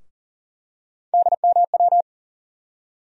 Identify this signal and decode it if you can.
Morse code, heard as a keyed tone, about 40 wpm, 700 Hz. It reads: DMW
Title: USA Call Sign Postfixes - 3 Letters 40wpm